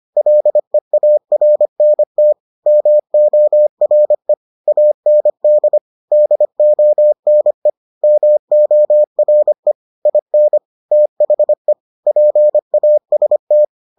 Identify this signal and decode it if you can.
Morse code, heard as a keyed tone, about 25 wpm, 600 Hz. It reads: LEARNT MORE AND DONE MORE IN THE PAST